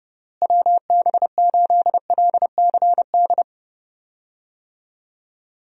Morse code, heard as a keyed tone, about 30 wpm, 700 Hz. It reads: WB8LCD
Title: USA Call Signs - 6 Characters - Repeated 2x - 30wpm